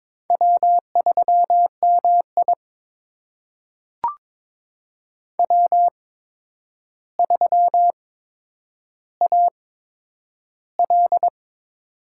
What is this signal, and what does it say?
Morse code, heard as a keyed tone, about 22 wpm, spaced as for 9 wpm, 700 Hz. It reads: W3MI W3AL
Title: USA Call Signs - 4 Characters - 8x Character Spacing - 22wpm